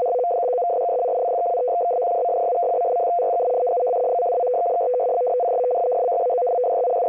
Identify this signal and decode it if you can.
Two-tone teleprinter signal, about 200 Hz shift.
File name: F03a- -1800-1805utc-7840khz